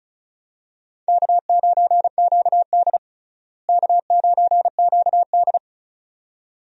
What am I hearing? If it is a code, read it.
Morse code, heard as a keyed tone, about 36 wpm, 700 Hz. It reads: K9QD K9QD
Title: USA Call Signs - 4 Characters - Repeated 2x - 35wpm